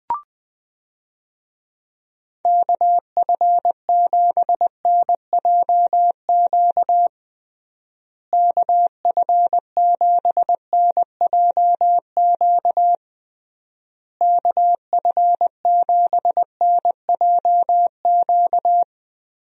Morse code, heard as a keyed tone, about 20 wpm, 700 Hz. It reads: KF7NJQ KF7NJQ KF7NJQ
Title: USA Call Signs - 6 Characters - Repeated 3x - 20wpm